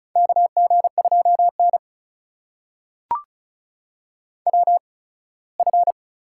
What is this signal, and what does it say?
Morse code, heard as a keyed tone, about 36 wpm, spaced as for 21 wpm, 700 Hz. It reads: KG2N WF
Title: USA Call Signs - 4 Characters - 8x Character Spacing - 35wpm